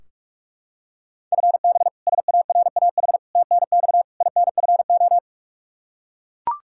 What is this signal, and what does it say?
Morse code, heard as a keyed tone, about 45 wpm, 700 Hz. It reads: FB SARAH TNX INFO
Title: K1USN SST QSO Practice 45wpm